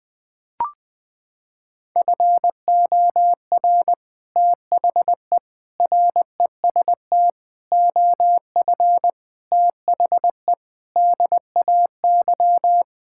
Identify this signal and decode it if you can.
Morse code, heard as a keyed tone, about 21 wpm, 700 Hz. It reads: FOR THE REST OF THE DAY
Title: Sentences with 3-Word Phrases - 20wpm